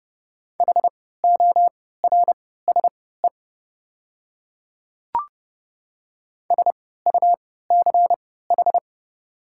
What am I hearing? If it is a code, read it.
Morse code, heard as a keyed tone, about 30 wpm, spaced as for 17 wpm, 690 Hz. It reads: HORSE SUCH
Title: 3x Character Spacing - Top 300 Words - 30wpm